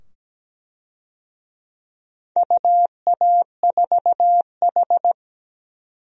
Morse code, heard as a keyed tone, about 17 wpm, 700 Hz. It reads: UA4H
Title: DXCC-100 Call Signs - 4 Characters - 17wpm